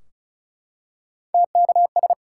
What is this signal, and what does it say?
Morse code, heard as a keyed tone, about 36 wpm, 700 Hz. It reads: TKS
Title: Top 180 QSO Elements 35wpm